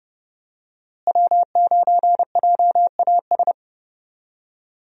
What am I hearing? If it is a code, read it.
Morse code, heard as a keyed tone, about 29 wpm, 700 Hz. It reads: W9JAS
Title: USA Call Signs - 5 Characters - Repeated 3x - 30wpm